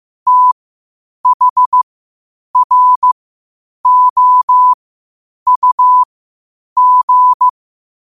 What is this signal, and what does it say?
Morse code, heard as a keyed tone, about 15 wpm, spaced as for 8 wpm, 1000 Hz. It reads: THROUG
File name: qotdfiles / 8wpm